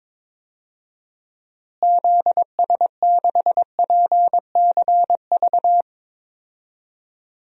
Morse code, heard as a keyed tone, about 21 wpm, 700 Hz. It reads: ZS6PCV
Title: DXCC-100 Call Signs - 6 Characters - Repeated 2x - 22wpm